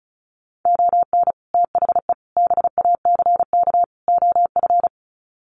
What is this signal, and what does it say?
Morse code, heard as a keyed tone, about 34 wpm, 700 Hz. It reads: ON THE BACK OF
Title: Sets of 4 Words 35wpm